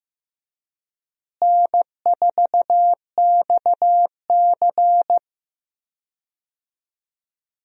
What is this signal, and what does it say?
Morse code, heard as a keyed tone, about 15 wpm, 700 Hz. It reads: N4XC
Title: USA Call Signs - 4 Characters - Repeated 3x - 15wpm